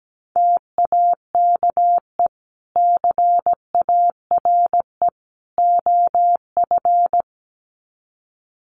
Morse code, heard as a keyed tone, about 17 wpm, 700 Hz. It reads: TAKE CARE OF
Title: Sets of 3 Words 17wpm